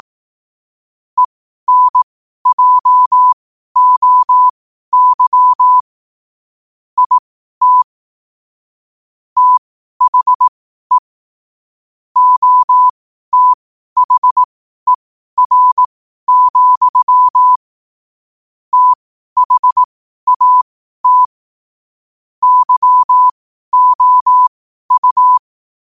Morse code, heard as a keyed tone, about 18 wpm, spaced as for 12 wpm, 1000 Hz. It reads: ENJOY IT THE OTHER, THAT YOU
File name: qotdfiles / 12wpm